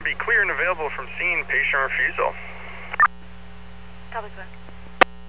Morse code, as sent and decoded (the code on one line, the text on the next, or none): none